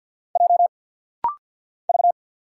Morse code, heard as a keyed tone, about 48 wpm, 700 Hz. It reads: J V
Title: Single Letter-Number - Rapid-Fire 50wpm